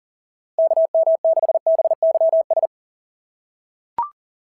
Morse code, heard as a keyed tone, about 40 wpm, 690 Hz. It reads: KM6BYS
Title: SOTA QSO Practice 40wpm